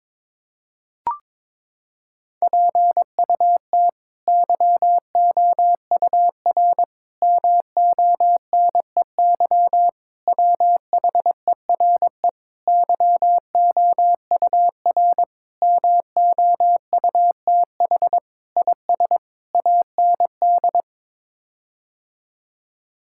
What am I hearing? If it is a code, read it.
Morse code, heard as a keyed tone, about 22 wpm, 700 Hz. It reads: PUT YOUR MONEY WHERE YOUR MOUTH IS AND
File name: practice / 22wpm / Sets of 8 Words - 22wpm